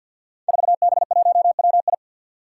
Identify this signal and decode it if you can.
Morse code, heard as a keyed tone, about 47 wpm, 700 Hz. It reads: 4B1WI
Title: Five Challenging Letter-Number Combinations 50wpm